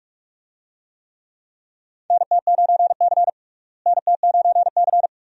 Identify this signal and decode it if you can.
Morse code, heard as a keyed tone, about 43 wpm, 700 Hz. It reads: NT9C NT9C
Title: USA Call Signs - 4 Characters - Repeated 2x - 45wpm